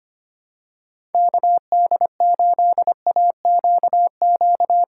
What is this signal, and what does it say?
Morse code, heard as a keyed tone, about 25 wpm, 700 Hz. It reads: KD8AQQ
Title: USA Call Signs - 6 Characters - 2x Character Spacing - 25wpm